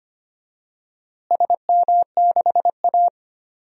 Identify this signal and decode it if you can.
Morse code, heard as a keyed tone, about 25 wpm, 700 Hz. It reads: SM6A
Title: DXCC-100 Call Signs - 4 Characters - Repeated 2x - 25wpm